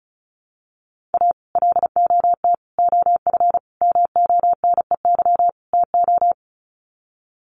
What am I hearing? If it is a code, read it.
Morse code, heard as a keyed tone, about 34 wpm, 700 Hz. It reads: A LOT OF MONEY TO
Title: Sets of 5 Words Encore2 35wpm